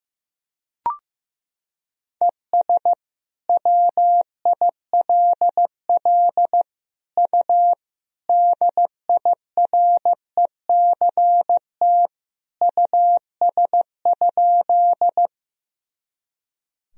Morse code, heard as a keyed tone, about 16 wpm, 700 Hz. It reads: ES WILL U DIRECT US?
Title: Sentences from Top 400 Words and Abbreviations 15wpm